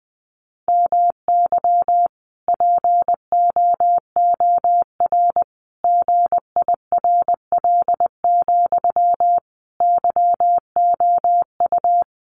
Morse code, heard as a keyed tone, about 20 wpm, 700 Hz. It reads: MY POOR GIRL, YOU
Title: Sentences from Top 1000 Words 20wpm